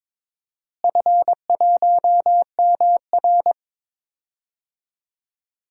Morse code, heard as a keyed tone, about 22 wpm, 700 Hz. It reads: F1MR
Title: DXCC-100 Call Signs - 4 Characters - Repeated 2x - 22wpm